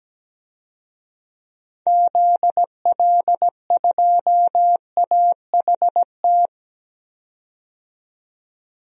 Morse code, heard as a keyed tone, about 17 wpm, 700 Hz. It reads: ZL2AHT
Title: DXCC-100 Call Signs - 6 Characters - 17wpm